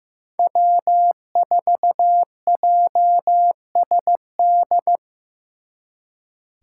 Morse code, heard as a keyed tone, about 18 wpm, 700 Hz. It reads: W4JSD
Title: Speed Racing - USA 1x3 Call Signs 15wpm